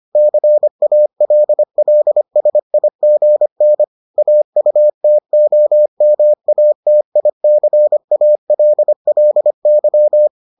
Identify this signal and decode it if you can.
Morse code, heard as a keyed tone, about 25 wpm, 600 Hz. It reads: CALLSIGN AUTOMATICALLY